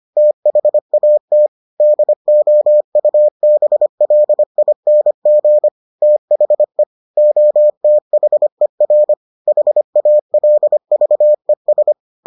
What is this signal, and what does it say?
Morse code, heard as a keyed tone, about 25 wpm, 600 Hz. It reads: THAT DOUBLING THE OTHER HALVES